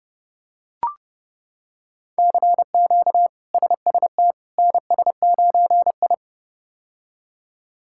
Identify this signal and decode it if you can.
Morse code, heard as a keyed tone, about 30 wpm, 700 Hz. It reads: CQ SST NS9I
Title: K1USN SST QSO Practice - 30wpm